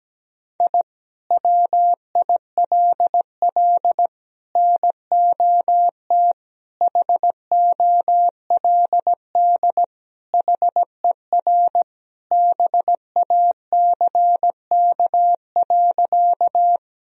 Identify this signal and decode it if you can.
Morse code, heard as a keyed tone, about 17 wpm, 700 Hz. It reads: I WILL NOT HOLD HER BACK.
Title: Sentences from Top 1000 Words - Review Part 2 - 17wpm